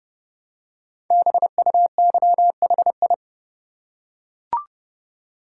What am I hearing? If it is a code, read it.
Morse code, heard as a keyed tone, about 30 wpm, 700 Hz. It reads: BUYHI
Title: Code Groups - 5 Letters - 30wpm